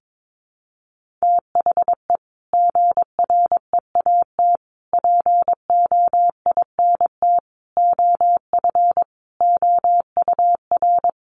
Morse code, heard as a keyed tone, about 22 wpm, 700 Hz. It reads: THE GREAT POINT OF OUR
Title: Sentences from Top 1500 Words 22wpm